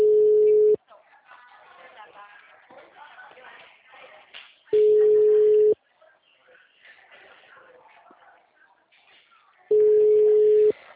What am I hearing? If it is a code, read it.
Ringback tone (Europe, 425 Hz), with speech.